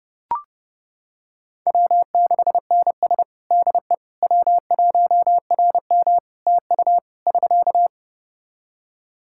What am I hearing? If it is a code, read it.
Morse code, heard as a keyed tone, about 30 wpm, 700 Hz. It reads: W6NS DE W1RM TU <SK>
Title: Practice QSO - Exchange 4 - Ending - 30wpm